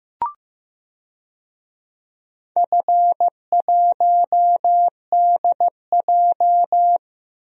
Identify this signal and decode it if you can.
Morse code, heard as a keyed tone, about 16 wpm, 700 Hz. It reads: F1DJ
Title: DXCC-100 Call Signs - 4 Characters - Repeated 3x - 15wpm